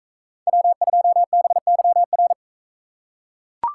Morse code, heard as a keyed tone, about 39 wpm, 700 Hz. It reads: W2BYR